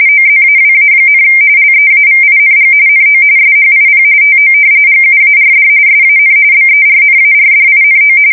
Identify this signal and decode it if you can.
RTTY, about 45.45 baud. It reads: OZAWA, LEAVING NO SHIP TO GUARD SAN BERNARDINO
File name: fsk170 / battleshipnj